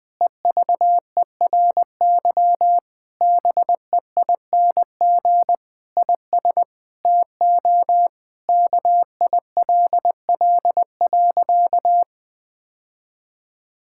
Morse code, heard as a keyed tone, about 20 wpm, 700 Hz. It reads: EVERY BEING IS TO KILL.
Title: Sentences from Top 1600 Words 20wpm